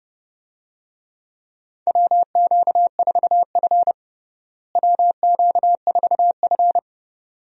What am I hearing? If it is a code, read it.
Morse code, heard as a keyed tone, about 31 wpm, 700 Hz. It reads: WQ4F WQ4F
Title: USA Call Signs - 4 Characters - Repeated 2x - 30wpm